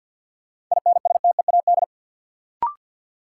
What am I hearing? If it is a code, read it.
Morse code, heard as a keyed tone, about 51 wpm, 700 Hz. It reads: INSTEAD
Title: Sentences from Top 800 Words - New Word Drill - 50wpm